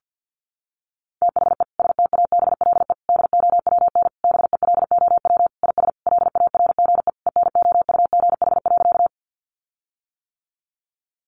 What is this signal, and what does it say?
Morse code, heard as a keyed tone, about 50 wpm, 700 Hz. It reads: THE STABLE DOWN BELOW IS LARGE ENOUGH.
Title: Sentences from Top 1500 Words 50wpm